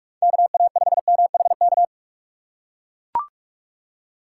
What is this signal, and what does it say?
Morse code, heard as a keyed tone, about 42 wpm, 700 Hz. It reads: KA5MHK
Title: USA Call Signs - 2x Character Spacing - 45wpm